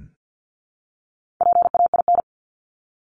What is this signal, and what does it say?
Morse code, heard as a keyed tone, about 50 wpm, 700 Hz. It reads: PAIN
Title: Top 1501-1750 Words 50wpm